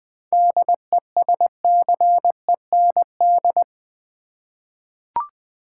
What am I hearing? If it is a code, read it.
Morse code, heard as a keyed tone, about 20 wpm, 700 Hz. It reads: DESCEND
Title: 7 Letter Words - First 3 Letters Given - 20wpm